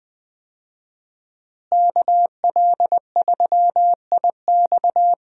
Morse code, heard as a keyed tone, about 20 wpm, 700 Hz. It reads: KL3IX
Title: USA Call Signs - 5 Characters - Repeated 3x - 20wpm